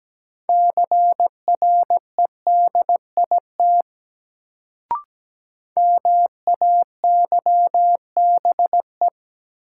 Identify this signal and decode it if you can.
Morse code, heard as a keyed tone, about 17 wpm, 700 Hz. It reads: CREDIT MAYBE
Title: Sentences from Top 1000 Words - New Word Drill - 17wpm